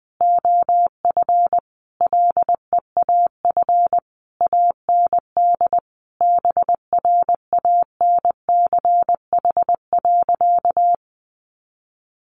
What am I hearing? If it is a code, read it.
Morse code, heard as a keyed tone, about 20 wpm, 700 Hz. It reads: OF LEAF AND BRANCH.
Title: A Sound Of Thunder 20wpm